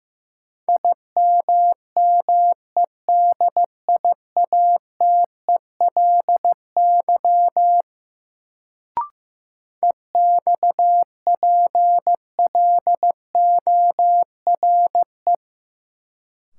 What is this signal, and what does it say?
Morse code, heard as a keyed tone, about 15 wpm, 700 Hz. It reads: IMMEDIATELY EXPLORE
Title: Sentences from Top 1100 Words - New Word Drill Alt - 15wpm